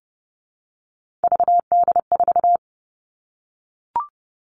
Morse code, heard as a keyed tone, about 30 wpm, 700 Hz. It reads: VD4
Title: All Cut Numbers - 3-Digit 30wpm